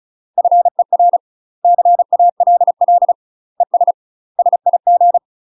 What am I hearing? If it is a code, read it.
Morse code, heard as a keyed tone, about 36 wpm, 700 Hz. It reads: FER CALL ES SIG
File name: UK MASTER QSOs TRAINING / 35WPM / UK MASTER QSOs 0002